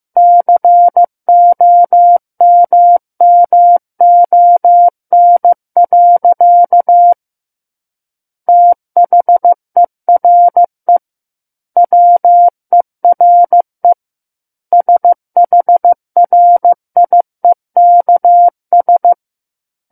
Morse code, heard as a keyed tone, about 15 wpm, 700 Hz. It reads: COMMON. THERE WERE SHRIEKS
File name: Book One / 06-Chapter 6 - 15 WPM